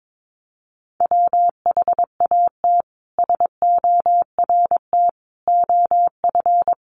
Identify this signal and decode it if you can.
Morse code, heard as a keyed tone, about 22 wpm, 700 Hz. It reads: WHAT SORT OF
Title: Sentences from Top 1300 Words 22wpm